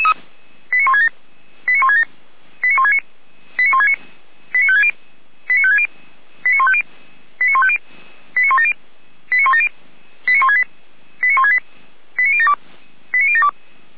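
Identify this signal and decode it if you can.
Five-tone selective call (ZVEI), heard as quick runs of five tones, tones of about 80 ms. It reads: 89167 89167 89169 89169 89560 89560 89150 89150 89170 89170 89168 89168 89072 89072
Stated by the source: ZVEI five-tone call received by SDR, feuerwehr-orlamuende.org fire brigade.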